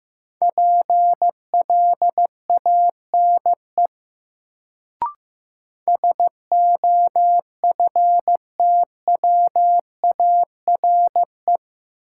Morse code, heard as a keyed tone, about 15 wpm, 700 Hz. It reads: PLANE SOFTWARE
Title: Sentences from Top 1000 Words - Review Part 3 - New Word Drill - 15wpm